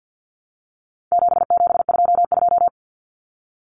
Morse code, heard as a keyed tone, about 50 wpm, 700 Hz. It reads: 7722
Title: Numbers - 4-Digit 50wpm